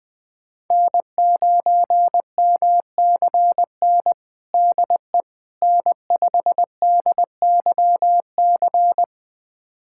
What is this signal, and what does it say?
Morse code, heard as a keyed tone, about 20 wpm, 700 Hz. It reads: N9MCN DE N5DYC
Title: USA Callsign DE Callsign - 5 Characters - 20wpm